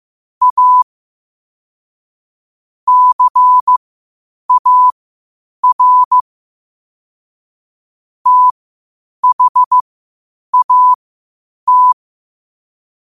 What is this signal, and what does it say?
Morse code, heard as a keyed tone, about 15 wpm, spaced as for 8 wpm, 1000 Hz. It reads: A CAR THAT
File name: qotdfiles / 8wpm